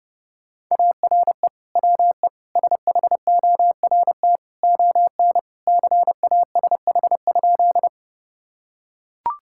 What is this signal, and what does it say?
Morse code, heard as a keyed tone, about 30 wpm, 700 Hz. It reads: ARE WE SHORT ON CASH?
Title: Sentences from Top 1000 Words - Review Part 5 - 30wpm